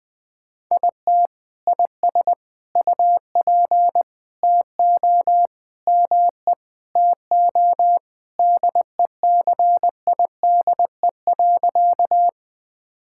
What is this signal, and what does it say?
Morse code, heard as a keyed tone, about 20 wpm, 700 Hz. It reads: IT IS UP TO ME TO DECIDE.
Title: Sentences with 2-Word Phrases - 20wpm